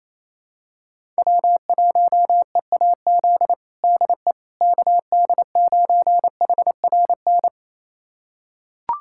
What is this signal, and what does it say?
Morse code, heard as a keyed tone, about 28 wpm, 700 Hz. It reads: W1EAZ DE KD9HRN
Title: USA Callsign DE Callsign 28wpm